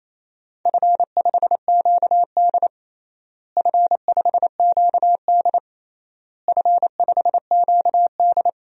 Morse code, heard as a keyed tone, about 28 wpm, 700 Hz. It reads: F5QD F5QD F5QD
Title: DXCC-100 Call Signs - 4 Characters - Repeated 3x - 28wpm